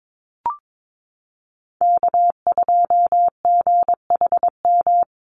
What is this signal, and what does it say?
Morse code, heard as a keyed tone, about 22 wpm, 700 Hz. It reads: K2GHM
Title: USA Call Signs Repeated 3x 22wpm